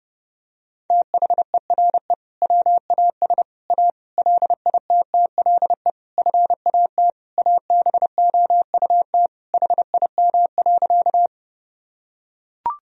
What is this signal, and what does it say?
Morse code, heard as a keyed tone, about 30 wpm, 700 Hz. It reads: THERE WAS A LITTLE FAT ABOUT HIM.
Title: Sentences from Top 1000 Words - Review Part 4 - 30wpm